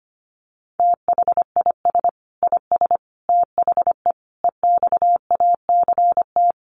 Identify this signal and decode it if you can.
Morse code, heard as a keyed tone, about 25 wpm, 700 Hz. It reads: THIS IS THE EXACT
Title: Sentences from Top 1300 Words 25wpm